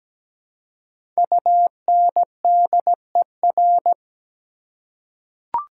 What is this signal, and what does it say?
Morse code, heard as a keyed tone, about 17 wpm, 700 Hz. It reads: UNDER
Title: Prefixes and Suffixes 17wpm